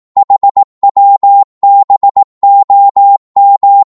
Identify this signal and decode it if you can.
Morse code, heard as a keyed tone, about 18 wpm, 800 Hz. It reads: HWBOM
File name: Morse / ltr-18wpm